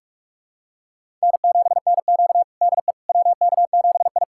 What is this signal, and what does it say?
Morse code, heard as a keyed tone, about 46 wpm, 700 Hz. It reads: N7NQ DE WK7I
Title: USA Callsign DE Callsign - 4 Characters - 45wpm